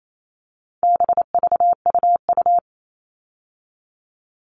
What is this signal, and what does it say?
Morse code, heard as a keyed tone, about 28 wpm, 700 Hz. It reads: BVUU
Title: All Cut Numbers - 4-Digit 28wpm